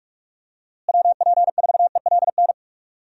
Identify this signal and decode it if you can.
Morse code, heard as a keyed tone, about 46 wpm, 700 Hz. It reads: WP4ELN
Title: USA Call Signs - 6 Characters - Repeated 3x - 45wpm